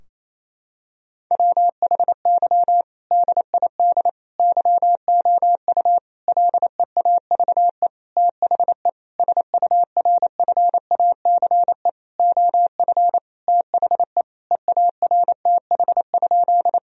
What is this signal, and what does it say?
Morse code, heard as a keyed tone, about 28 wpm, 700 Hz. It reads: WHY DID YOU LEAVE THE SURFACE OF THE EARTH?
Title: Sentences from Top 1000 Words - Review Part 3 - 28wpm